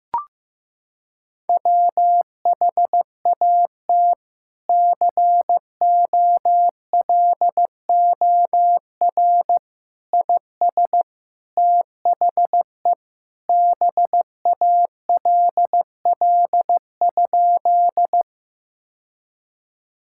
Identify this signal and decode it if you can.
Morse code, heard as a keyed tone, about 15 wpm, 700 Hz. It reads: WHAT COLOR IS THE BALL?
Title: Sentences from Top 1000 Words - Review Part 6 - 15wpm